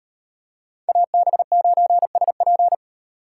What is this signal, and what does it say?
Morse code, heard as a keyed tone, about 38 wpm, 700 Hz. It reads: AB9SP